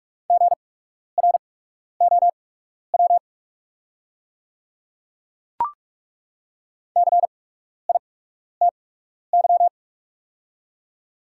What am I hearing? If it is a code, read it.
Morse code, heard as a keyed tone, about 48 wpm, spaced as for 12 wpm, 690 Hz. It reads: GROW CITY
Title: 8x Character Spacing - Top 300 Words - 45wpm